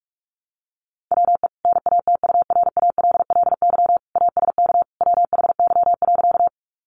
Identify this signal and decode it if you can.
Morse code, heard as a keyed tone, about 44 wpm, 700 Hz. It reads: WE NATURALLY ASK WHY.
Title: Sentences from Top 1300 Words 45wpm